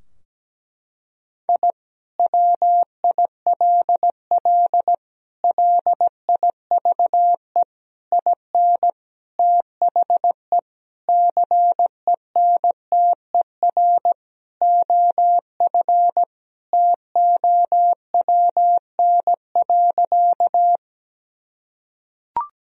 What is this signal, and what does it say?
Morse code, heard as a keyed tone, about 17 wpm, 700 Hz. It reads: I WILL LIVE IN THE CENTER OF TOWN.
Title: Sentences from Top 1000 Words - Review Part 2 - 17wpm